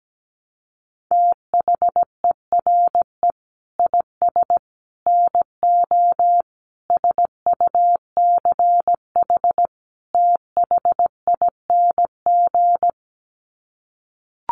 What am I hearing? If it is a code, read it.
Morse code, heard as a keyed tone, about 17 wpm, 700 Hz. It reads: THERE IS NO SUCH THING
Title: Sets of 5 Words Encore 17wpm